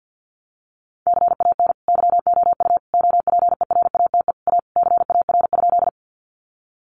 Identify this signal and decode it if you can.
Morse code, heard as a keyed tone, about 50 wpm, 700 Hz. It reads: CAN YOU OPERATE A CAR?
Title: Sentences from Top 1500 Words 50wpm